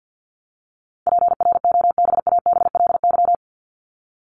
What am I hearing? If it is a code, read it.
Morse code, heard as a keyed tone, about 50 wpm, 700 Hz. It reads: PROBABLY
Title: Top 501-750 Words 50wpm